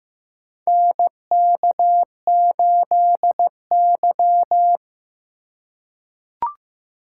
Morse code, heard as a keyed tone, about 15 wpm, 700 Hz. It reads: NK8Y
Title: USA Call Signs - 4 Characters - 6x Character Spacing - 15wpm